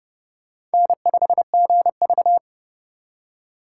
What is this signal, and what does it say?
Morse code, heard as a keyed tone, about 30 wpm, 700 Hz. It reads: N5GV
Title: USA Call Signs - 4 Characters - 2x Character Spacing - 30wpm